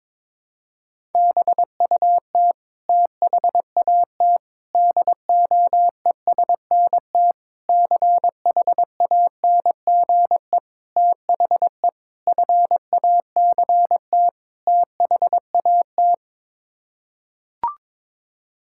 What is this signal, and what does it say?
Morse code, heard as a keyed tone, about 22 wpm, 700 Hz. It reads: BUT THAT DOESNT CHANGE THE FACT THAT
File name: practice / 22wpm / Sets of 7 Words - 22wpm